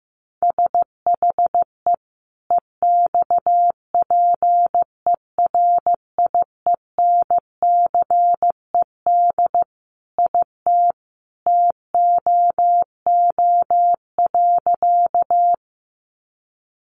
Morse code, heard as a keyed tone, about 15 wpm, 700 Hz. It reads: SHE EXPERIENCED IT TOO.
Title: Sentences from Top 1100 Words 15wpm